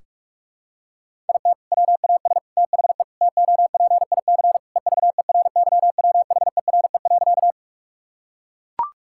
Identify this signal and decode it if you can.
Morse code, heard as a keyed tone, about 41 wpm, 700 Hz. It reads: IT WAS THE TOPIC EVERYWHERE.
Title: Sentences from Top 1600 Words 45wpm